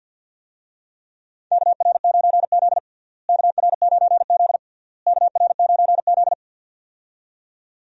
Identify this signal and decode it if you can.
Morse code, heard as a keyed tone, about 48 wpm, 700 Hz. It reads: KR9Z KR9Z KR9Z
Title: USA Call Signs - 4 Characters - Repeated 3x - 50wpm